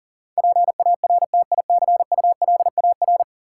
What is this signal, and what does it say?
Morse code, heard as a keyed tone, about 40 wpm, 700 Hz. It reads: PARTICULAR
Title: Sentences from Top 600 Words - New Word Drill - 40wpm